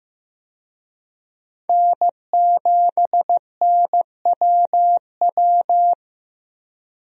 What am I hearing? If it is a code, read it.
Morse code, heard as a keyed tone, about 15 wpm, 700 Hz. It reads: N7NWW
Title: USA Call Signs - 5 Characters - 15wpm